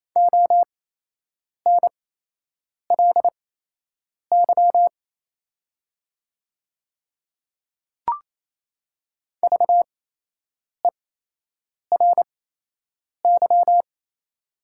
Morse code, heard as a keyed tone, about 28 wpm, spaced as for 8 wpm, 700 Hz. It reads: ONLY VERY
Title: 8x Character Spacing - Top 200 Words - 28wpm